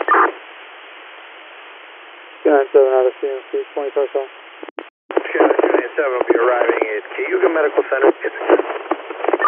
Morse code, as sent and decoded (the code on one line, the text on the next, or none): none